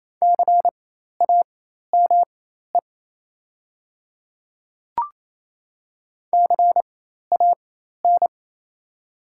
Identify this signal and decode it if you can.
Morse code, heard as a keyed tone, about 28 wpm, spaced as for 13 wpm, 700 Hz. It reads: CAME CAN
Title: 4x Character Spacing - Top 500 Words - 28wpm